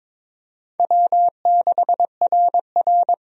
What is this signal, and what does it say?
Morse code, heard as a keyed tone, about 22 wpm, 700 Hz. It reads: W6RR
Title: USA Call Signs - 4 Characters - 22wpm